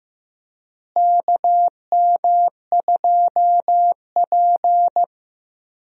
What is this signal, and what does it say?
Morse code, heard as a keyed tone, about 16 wpm, 700 Hz. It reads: KM2P
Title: USA Call Signs - 4 Characters - Repeated 3x - 15wpm